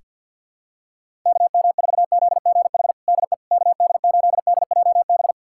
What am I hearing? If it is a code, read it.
Morse code, heard as a keyed tone, about 52 wpm, 700 Hz. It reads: KM4ZGH DE KD8DJB
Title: USA Callsign DE Callsign 50wpm